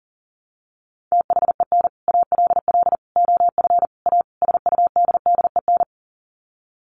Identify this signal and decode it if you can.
Morse code, heard as a keyed tone, about 40 wpm, 700 Hz. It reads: THEN ALL OF A SUDDEN
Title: Sets of 5 Words Encore 40wpm